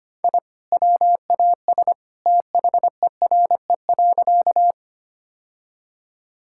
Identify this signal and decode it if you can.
Morse code, heard as a keyed tone, about 25 wpm, 700 Hz. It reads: I WAS THERE.
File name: practice / 25wpm / Sentences with 2-Word Phrases - 25wpm